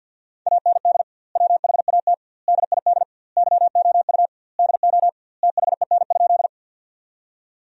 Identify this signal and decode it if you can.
Morse code, heard as a keyed tone, about 50 wpm, 700 Hz. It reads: AND WHAT DID YOU DO THEN?
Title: 2x Word-Spacing - Sentences from Top 100 Words - 50wpm